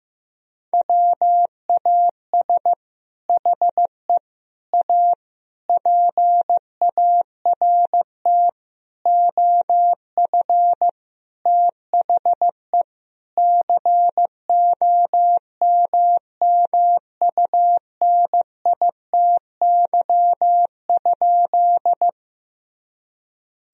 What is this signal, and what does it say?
Morse code, heard as a keyed tone, about 15 wpm, 700 Hz. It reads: WAS HE A PART OF THE COMMUNITY?
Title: Sentences from Top 1000 Words - Review Part 4 - 15wpm